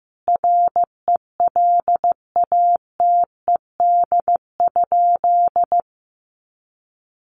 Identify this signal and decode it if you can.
Morse code, heard as a keyed tone, about 15 wpm, 700 Hz. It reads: RELATED?
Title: Sentences from Top 1400 Words 15wpm